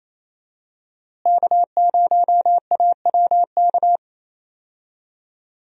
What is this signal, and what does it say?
Morse code, heard as a keyed tone, about 28 wpm, 700 Hz. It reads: K0AWK
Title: USA Call Signs - 5 Characters - Repeated 3x - 28wpm